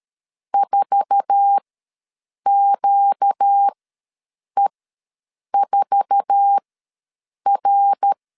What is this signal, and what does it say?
Morse code, heard as a keyed tone, about 13 wpm, spaced as for 7 wpm, 800 Hz. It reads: 4QE4R
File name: LEZIONI / 8 YJ45 / 8 YJ4 ws 12wpm 3249